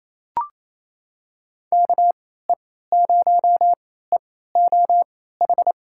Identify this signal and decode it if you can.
Morse code, heard as a keyed tone, about 28 wpm, spaced as for 16 wpm, 700 Hz. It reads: KE0EOH
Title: USA Call Signs - 6 Characters - 3x Character Spacing - 28wpm